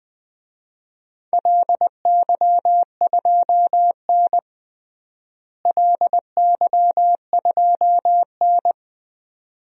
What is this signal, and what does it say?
Morse code, heard as a keyed tone, about 20 wpm, 700 Hz. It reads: LY2N LY2N
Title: DXCC-100 Call Signs - 4 Characters - Repeated 2x - 20wpm